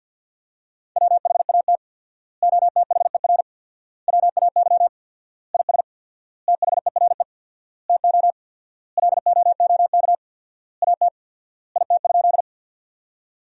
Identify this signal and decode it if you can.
Morse code, heard as a keyed tone, about 50 wpm, 700 Hz. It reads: WHAT OTHER WAY IS THERE TO LOOK AT IT?
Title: 4x Word-Spacing - Sentences from Top 100 Words - 50wpm